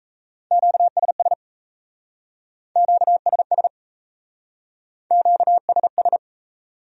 Morse code, heard as a keyed tone, about 37 wpm, 700 Hz. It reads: QSS QSS QSS